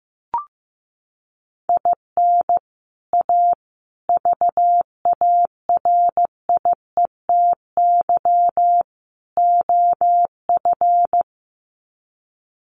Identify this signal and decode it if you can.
Morse code, heard as a keyed tone, about 15 wpm, 700 Hz. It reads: IN A VARIETY OF
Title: Sets of 4 Words 15wpm